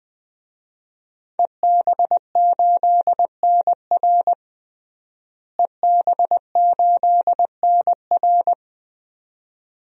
Morse code, heard as a keyed tone, about 20 wpm, 700 Hz. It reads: EB8NR EB8NR
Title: DXCC-100 Call Signs - 5 Characters - Repeated 2x - 20wpm